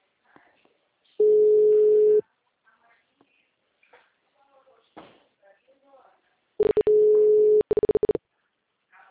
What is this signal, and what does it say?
Ringback tone (Europe, 425 Hz), with speech.
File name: out- -2008- - - .3981